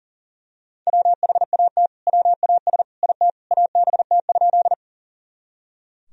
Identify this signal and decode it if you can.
Morse code, heard as a keyed tone, about 40 wpm, 700 Hz. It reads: WHAT WAS IT ABT?
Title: Sentences from Top 100 Words and Abbreviations 40wpm